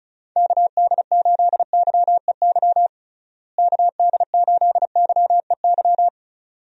Morse code, heard as a keyed tone, about 36 wpm, 700 Hz. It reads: KD8YEY KD8YEY
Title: USA Call Signs - 6 Characters - Repeated 2x - 35wpm